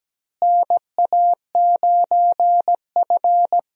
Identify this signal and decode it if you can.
Morse code, heard as a keyed tone, about 17 wpm, 700 Hz. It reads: NA9F
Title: USA Call Signs - 4 Characters - Repeated 2x - 17wpm